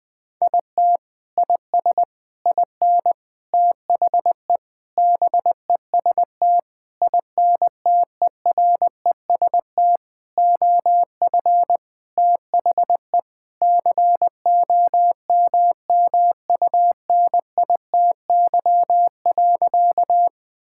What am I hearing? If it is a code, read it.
Morse code, heard as a keyed tone, about 20 wpm, 700 Hz. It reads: IT IS IN THE BEST INTEREST OF THE COMMUNITY.
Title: Sentences with 6-Word Phrases - Spoken Beginning - 20wpm